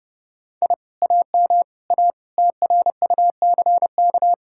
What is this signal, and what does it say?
Morse code, heard as a keyed tone, about 29 wpm, 700 Hz. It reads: I AM A TRUCK